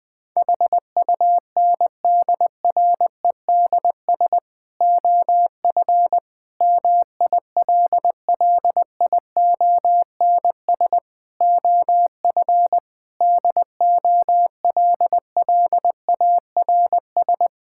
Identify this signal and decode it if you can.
Morse code, heard as a keyed tone, about 20 wpm, 700 Hz. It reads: HUNDREDS OF MILLIONS OF DOLLARS
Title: Sentences with 5-Word Phrases - 20wpm